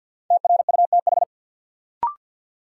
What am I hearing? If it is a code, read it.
Morse code, heard as a keyed tone, about 51 wpm, 700 Hz. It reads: TRUTH
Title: Sentences from Top 800 Words - New Word Drill - 50wpm